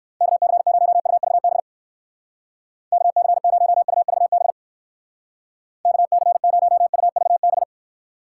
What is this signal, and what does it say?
Morse code, heard as a keyed tone, about 60 wpm, 700 Hz. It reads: KC0UVB KC0UVB KC0UVB